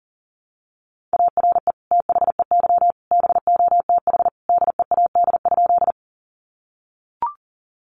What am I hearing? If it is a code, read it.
Morse code, heard as a keyed tone, about 40 wpm, 700 Hz. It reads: ARE THEY BOTH DEAD?
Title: Sentences from Top 1500 Words 40wpm